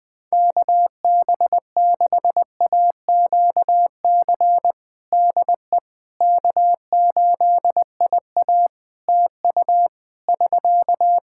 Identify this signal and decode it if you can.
Morse code, heard as a keyed tone, about 20 wpm, 700 Hz. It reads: KB6AQC DE K8IA TU <SK>
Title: K1USN SST QSO Practice 20wpm